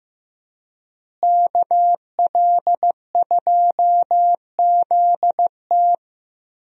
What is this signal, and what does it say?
Morse code, heard as a keyed tone, about 15 wpm, 700 Hz. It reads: KL2ZT
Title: USA Call Signs - 5 Characters - Repeated 2x - 15wpm